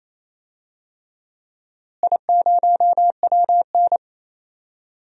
Morse code, heard as a keyed tone, about 28 wpm, 700 Hz. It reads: I0WN
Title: DXCC-100 Call Signs - 4 Characters - Repeated 3x - 28wpm